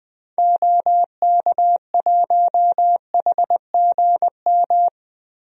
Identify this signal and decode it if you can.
Morse code, heard as a keyed tone, about 20 wpm, 700 Hz. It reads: OK1HGM
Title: DXCC-100 Call Signs - 6 Characters - Repeated 2x - 20wpm